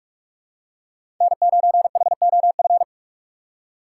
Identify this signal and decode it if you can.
Morse code, heard as a keyed tone, about 46 wpm, 700 Hz. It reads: N9HOF
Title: USA Call Signs - 5 Characters - 2x Character Spacing - 45wpm